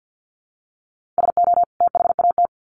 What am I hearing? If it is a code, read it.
Morse code, heard as a keyed tone, about 50 wpm, 700 Hz. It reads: SO THAT
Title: Sets of 2 Words 50wpm